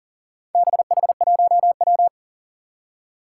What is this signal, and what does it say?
Morse code, heard as a keyed tone, about 40 wpm, 700 Hz. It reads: BH1W
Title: Four Challenging Letter-Number Combinations 40wpm